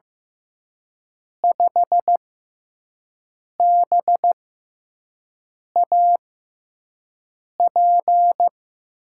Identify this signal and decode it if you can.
Morse code, heard as a keyed tone, about 15 wpm, spaced as for 5 wpm, 700 Hz. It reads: 5BAP
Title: USA Call Signs - 5 Characters - 6x Character Spacing - 15wpm